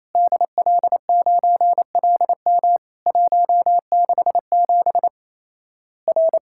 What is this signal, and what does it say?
Morse code, heard as a keyed tone, about 27 wpm, 700 Hz. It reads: DL9LM 167 R
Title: Worked All Europe DX Contest - QTC Exchange - 28wpm